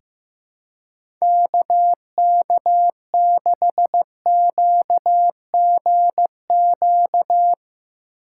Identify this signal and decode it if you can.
Morse code, heard as a keyed tone, about 15 wpm, 700 Hz. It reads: KK6QGQ
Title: USA Call Signs - 6 Characters - 2x Character Spacing - 15wpm